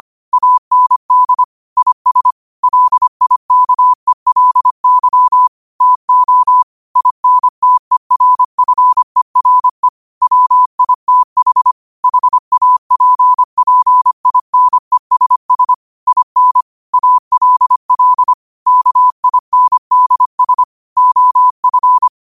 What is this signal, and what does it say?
Morse code, heard as a keyed tone, about 25 wpm, 1000 Hz. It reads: AND IS LIKELY TO INTERFERE WITH HAPPINESS IN ALL KINDS OF
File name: qotdfiles / 25wpm